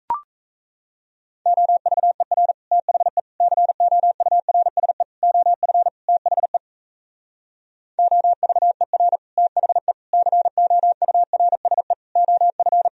Morse code, heard as a keyed tone, about 37 wpm, 700 Hz. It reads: OVER THE COURSE OF THE OVER THE COURSE OF